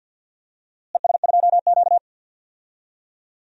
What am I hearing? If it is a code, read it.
Morse code, heard as a keyed tone, about 47 wpm, 700 Hz. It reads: ES2Q
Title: DXCC-100 Call Signs - 4 Characters - Repeated 2x - 50wpm